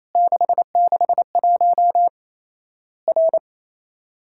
Morse code, heard as a keyed tone, about 27 wpm, 700 Hz. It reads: 661 R
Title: Worked All Europe DX Contest - QTC Exchange - 28wpm